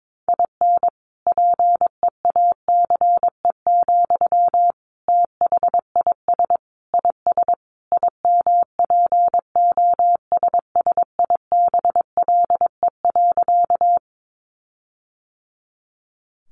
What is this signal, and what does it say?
Morse code, heard as a keyed tone, about 22 wpm, 700 Hz. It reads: IN PEACE, THIS IS IMPOSSIBLE.
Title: Sentences from Top 1500 Words 22wpm